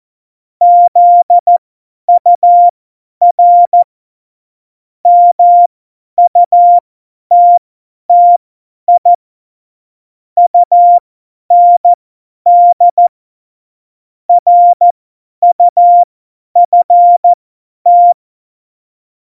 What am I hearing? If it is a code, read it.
Morse code, heard as a keyed tone, about 14 wpm, spaced as for 10 wpm, 700 Hz. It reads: ZUR MUTTI UND RUFT
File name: Morsekurs / Witze / Witz 24. Oktober 17, Tempo 50